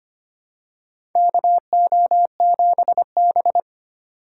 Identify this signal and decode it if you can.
Morse code, heard as a keyed tone, about 25 wpm, 700 Hz. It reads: KO7B
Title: USA Call Signs - 4 Characters - 4x Character Spacing - 25wpm